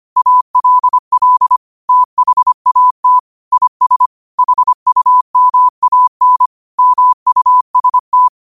Morse code, heard as a keyed tone, about 25 wpm, 1000 Hz. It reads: ALL THAT IS HUMAN MUST